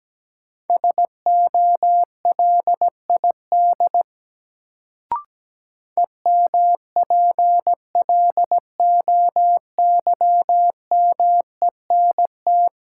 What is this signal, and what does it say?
Morse code, heard as a keyed tone, about 17 wpm, 700 Hz. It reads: SOLID EMPLOYMENT
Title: Sentences from Top 1500 Words - New Word Drill Alt - 17wpm